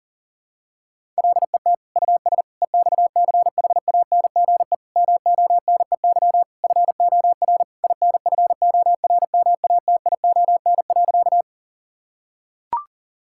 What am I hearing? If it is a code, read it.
Morse code, heard as a keyed tone, about 40 wpm, 700 Hz. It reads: LET US EXCHANGE MONEY FOR INFORMATION.
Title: Sentences from Top 1000 Words - Review Part 6 - 40wpm